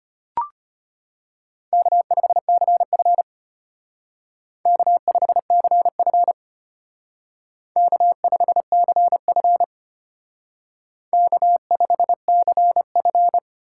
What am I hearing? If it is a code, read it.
Morse code, heard as a keyed tone, about 32 wpm, 700 Hz. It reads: K5CF K5CF K5CF K5CF